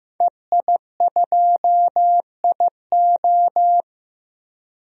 Morse code, heard as a keyed tone, about 15 wpm, 700 Hz. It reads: EI2IO
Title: DXCC-100 Call Signs - 5 Characters - 15wpm